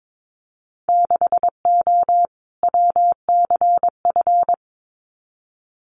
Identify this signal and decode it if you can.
Morse code, heard as a keyed tone, about 22 wpm, 700 Hz. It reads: 6O WCF
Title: Winter Field Day Exchanges Encore 22wpm